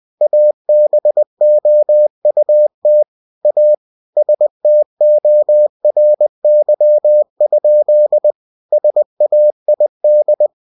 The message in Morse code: ABOUT A STORY? SAID

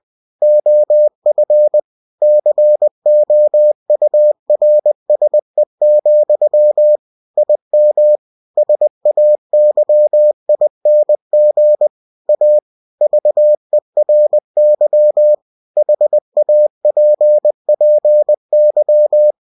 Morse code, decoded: OF COURSE, IM SAYING A VERY HAPPY

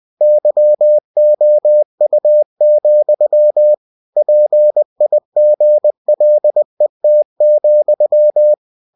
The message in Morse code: YOU, PIGLET,

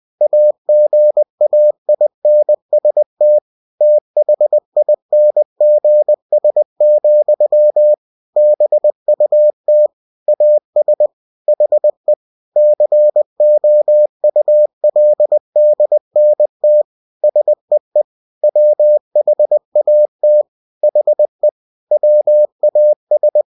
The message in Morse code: AGAINST THINGS, BUT AS HE COULDNT SEE WHAT HE WAS

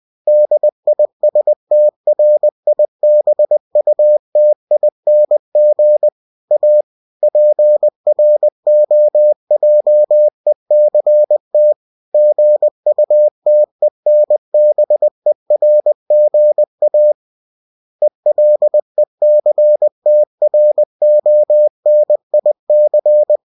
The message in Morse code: DISTRIBUTING A PROJECT GUTENBERGA ELECTRONIC

